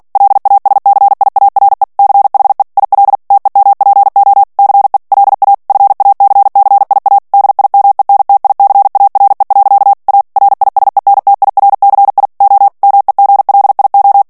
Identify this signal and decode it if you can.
Morse code, heard as a keyed tone, about 48 wpm, 800 Hz. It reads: LASCIARE CHE IL TEMPO CE LA FACCIA DIMENTICARE. A RISENTIRCI O MEGLIO
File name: morse mp3 / ITALIEN texte clair 1 vitesse 45 mots minute